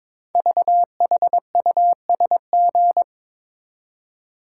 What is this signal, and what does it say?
Morse code, heard as a keyed tone, about 22 wpm, 700 Hz. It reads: VHUSG
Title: Code Groups - 5 Letters - 22wpm